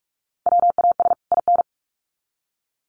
Morse code, heard as a keyed tone, about 44 wpm, 700 Hz. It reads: WAS IN
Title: Sets of 2 Words 45wpm